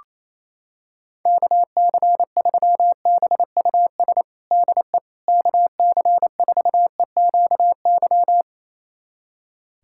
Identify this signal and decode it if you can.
Morse code, heard as a keyed tone, about 28 wpm, 700 Hz. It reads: KC3BUS DE KC4EQY
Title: USA Callsign DE Callsign 28wpm